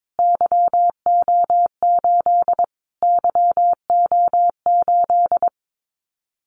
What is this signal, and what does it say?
Morse code, heard as a keyed tone, about 22 wpm, 700 Hz. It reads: YO8 YO8
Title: Callsign Prefixes for Top 25 DXCC - Repeated 3x - 22wpm